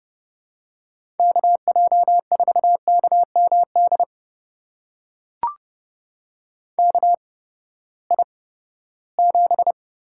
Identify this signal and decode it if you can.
Morse code, heard as a keyed tone, about 30 wpm, spaced as for 17 wpm, 700 Hz. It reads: KJ4KMD KI7
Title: USA Call Signs - 6 Characters - 8x Character Spacing - 30wpm